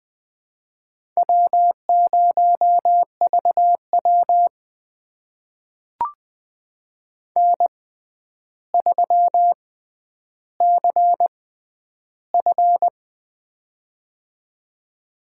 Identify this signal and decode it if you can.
Morse code, heard as a keyed tone, about 20 wpm, spaced as for 10 wpm, 700 Hz. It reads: W0VW N3CF
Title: USA Call Signs - 4 Characters - 6x Character Spacing - 20wpm